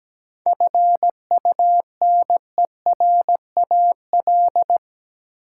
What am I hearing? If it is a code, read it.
Morse code, heard as a keyed tone, about 17 wpm, 700 Hz. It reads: FUNERAL
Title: Seven Letter Words - 17wpm